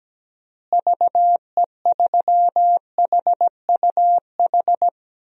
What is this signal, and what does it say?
Morse code, heard as a keyed tone, about 17 wpm, 700 Hz. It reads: VE3HUH
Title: DXCC-100 Call Signs - 6 Characters - Repeated 3x - 17wpm